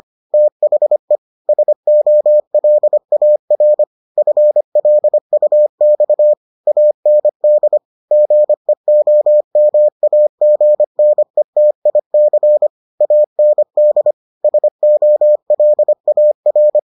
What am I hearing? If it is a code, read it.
Morse code, heard as a keyed tone, about 25 wpm, 600 Hz. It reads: THE SOLAR FLUX AND GEOMAGNETIC AND SOLAR